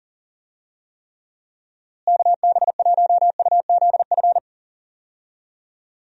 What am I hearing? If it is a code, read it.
Morse code, heard as a keyed tone, about 40 wpm, 700 Hz. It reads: KB1UZF
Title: USA Call Signs - 6 Characters - Repeated 2x - 40wpm